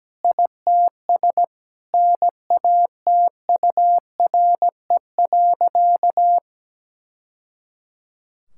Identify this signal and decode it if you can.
Morse code, heard as a keyed tone, about 17 wpm, 700 Hz. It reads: ITS NATURE.
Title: Sentences from Top 1000 Words - Review Part 6 - 17wpm